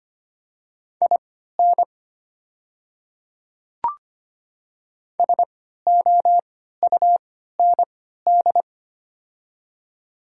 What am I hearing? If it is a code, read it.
Morse code, heard as a keyed tone, about 25 wpm, spaced as for 14 wpm, 700 Hz. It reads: IN SOUND
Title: 3x Character Spacing - Top 200 Words - 25wpm